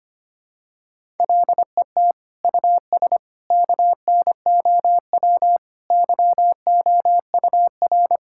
Morse code, heard as a keyed tone, about 25 wpm, 700 Hz. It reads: LET US KNOW YOUR
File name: Sets of 9 Words - 25wpm